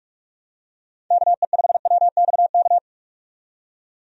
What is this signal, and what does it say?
Morse code, heard as a keyed tone, about 46 wpm, 700 Hz. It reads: KE5WXK
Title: USA Call Signs - 6 Characters - 45wpm